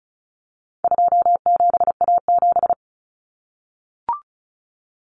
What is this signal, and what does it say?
Morse code, heard as a keyed tone, about 34 wpm, 700 Hz. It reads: 27A7
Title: Common Cut Numbers - 4-Digit 35wpm